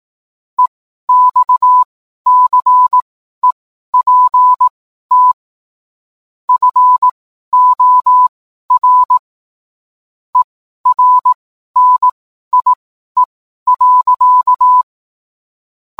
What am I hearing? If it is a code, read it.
Morse code, heard as a keyed tone, about 18 wpm, spaced as for 12 wpm, 1000 Hz. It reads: EXCEPT FOR ERNIE.